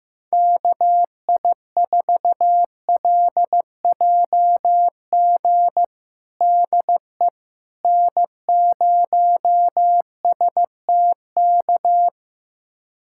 Morse code, heard as a keyed tone, about 16 wpm, 700 Hz. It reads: KI4LJG DE N0STK
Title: USA Callsign DE Callsign 15wpm